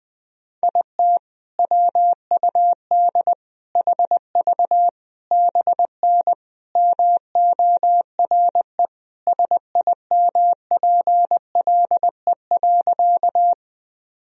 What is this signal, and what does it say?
Morse code, heard as a keyed tone, about 20 wpm, 700 Hz. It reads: IT WUD HV BN MORE SIMPLE.
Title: Sentences from Top 400 Words and Abbreviations 20wpm